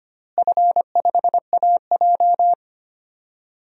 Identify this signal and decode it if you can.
Morse code, heard as a keyed tone, about 25 wpm, 700 Hz. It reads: F5AJ
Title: DXCC-100 Call Signs - 4 Characters - Repeated 3x - 25wpm